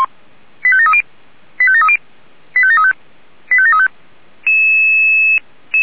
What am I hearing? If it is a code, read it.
Five-tone selective call (ZVEI), heard as quick runs of five tones, tones of about 70 ms. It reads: 85730 85730 85735 85735